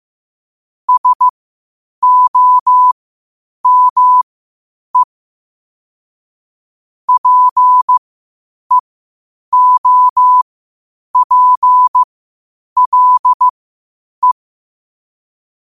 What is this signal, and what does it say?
Morse code, heard as a keyed tone, about 15 wpm, spaced as for 8 wpm, 1000 Hz. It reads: SOME PEOPLE
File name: qotdfiles / 8wpm